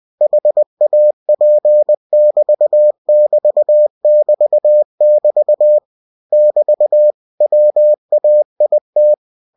Morse code, heard as a keyed tone, about 20 wpm, 600 Hz. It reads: HAP<BT><BT><BT><BT> <BT> WAIT
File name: cw-books / winnie-the-pooh / winnie-20wpm-page-0054